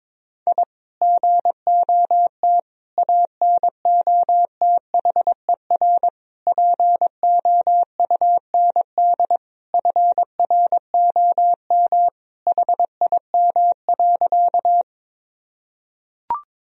Morse code, heard as a keyed tone, about 22 wpm, 700 Hz. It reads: I GOT ANOTHER POUND FROM HIM.
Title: Sentences from Top 1000 Words - Review Part 2 - 22wpm